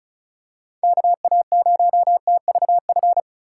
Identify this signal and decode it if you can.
Morse code, heard as a keyed tone, about 36 wpm, 700 Hz. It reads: KA0TVF
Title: USA Call Signs - 6 Characters - 4x Character Spacing - 35wpm